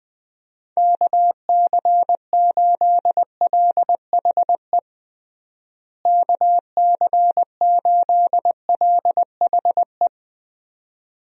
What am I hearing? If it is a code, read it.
Morse code, heard as a keyed tone, about 20 wpm, 700 Hz. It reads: KC8LHE KC8LHE
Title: USA Call Signs - 6 Characters - Repeated 2x - 20wpm